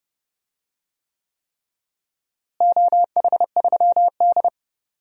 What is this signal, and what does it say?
Morse code, heard as a keyed tone, about 30 wpm, 700 Hz. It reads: OH3D
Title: DXCC-100 Call Signs - 4 Characters - Repeated 3x - 30wpm